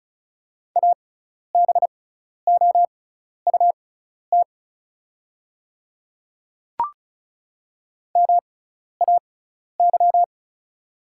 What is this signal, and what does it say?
Morse code, heard as a keyed tone, about 36 wpm, spaced as for 12 wpm, 690 Hz. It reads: ABOUT MAY
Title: 6x Character Spacing - Top 100 Words - 35wpm